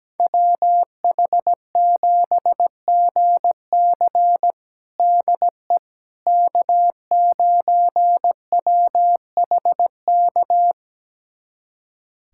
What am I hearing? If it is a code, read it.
Morse code, heard as a keyed tone, about 17 wpm, 700 Hz. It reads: WH7GC DE K9WHK
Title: USA Callsign DE Callsign - 5 Characters - 17wpm